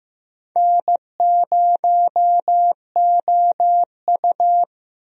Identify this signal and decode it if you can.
Morse code, heard as a keyed tone, about 15 wpm, 700 Hz. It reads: N0OU
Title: USA Call Signs - 4 Characters - Repeated 3x - 15wpm